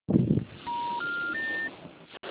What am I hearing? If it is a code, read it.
Special information tone, with noise.
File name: out- -2008- - - .2